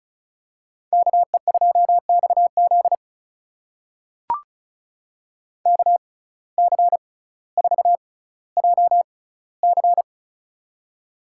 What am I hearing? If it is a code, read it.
Morse code, heard as a keyed tone, about 36 wpm, spaced as for 18 wpm, 700 Hz. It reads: KE2XZ KC4JC
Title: USA Call Signs - 6x Character Spacing - 35wpm